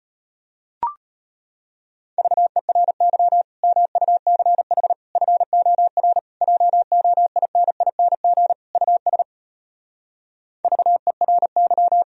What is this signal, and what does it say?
Morse code, heard as a keyed tone, about 33 wpm, 700 Hz. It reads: VERY MUCH FOR JOINING US VERY